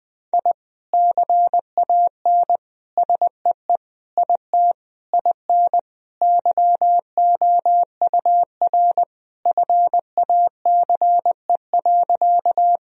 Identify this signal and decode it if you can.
Morse code, heard as a keyed tone, about 20 wpm, 700 Hz. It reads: I CAN SEE IT IN YOUR FACE.
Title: Sentences from Top 1000 Words - Review Part 2 - 20wpm